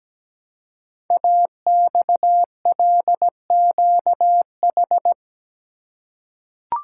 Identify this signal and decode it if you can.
Morse code, heard as a keyed tone, about 17 wpm, 700 Hz. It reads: AXLQH
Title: Code Groups - 5 Letters - 17wpm